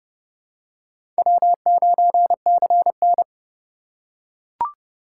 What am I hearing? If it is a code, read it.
Morse code, heard as a keyed tone, about 28 wpm, 700 Hz. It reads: W9CN
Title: USA Call Signs - 4 Characters - 3x Character Spacing - 30wpm